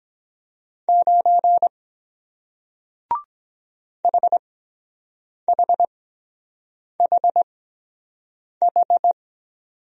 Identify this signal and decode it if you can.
Morse code, heard as a keyed tone, about 22 wpm, spaced as for 7 wpm, 700 Hz. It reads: 9 HHHH